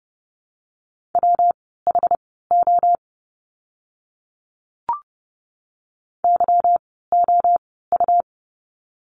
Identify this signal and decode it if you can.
Morse code, heard as a keyed tone, about 30 wpm, spaced as for 17 wpm, 700 Hz. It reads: WHO YOU
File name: course / 30:10wpm / 237 Lesson 38f - Letter q - Words 30wpm